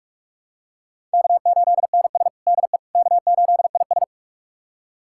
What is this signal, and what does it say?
Morse code, heard as a keyed tone, about 41 wpm, 700 Hz. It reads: K8NS DE K8IS
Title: USA Callsign DE Callsign - 4 Characters - 45wpm